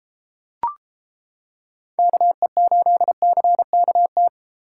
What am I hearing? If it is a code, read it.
Morse code, heard as a keyed tone, about 32 wpm, 700 Hz. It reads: KE8CKT